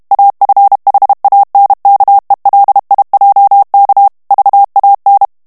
Morse code, heard as a keyed tone, about 32 wpm, 800 Hz. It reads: AFHANKELIJK VAN
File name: HOLLANDAIS texte clair 1 vitesse 30 mots minute